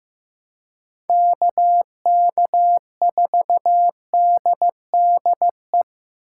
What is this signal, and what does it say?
Morse code, heard as a keyed tone, about 15 wpm, 700 Hz. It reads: KK4DDE
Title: USA Call Signs - 6 Characters - 4x Character Spacing - 15wpm